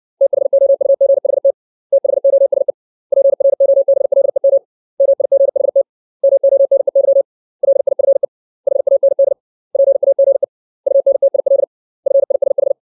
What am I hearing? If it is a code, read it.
Morse code, heard as a keyed tone, about 60 wpm, 550 Hz. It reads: THOUGHT THOSE PROBLEM MIGHT MONEY LIFE HAND PAGE UNTIL FISH
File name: 632 words / 632 Words 60WPM